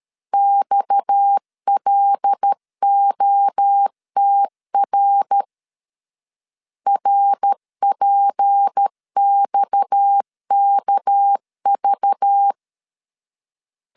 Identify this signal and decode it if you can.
Morse code, heard as a keyed tone, about 13 wpm, 800 Hz. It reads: XLOTR RPXKV
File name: LEZIONI / 6 PKX0 / 6 PKX 12wpm 2024